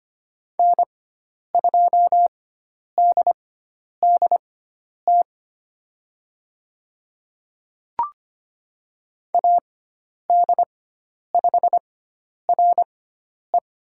Morse code, heard as a keyed tone, about 25 wpm, spaced as for 10 wpm, 700 Hz. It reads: N2DDT AD5RE
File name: Lesson 16g - Letter c - Call Signs 25wpm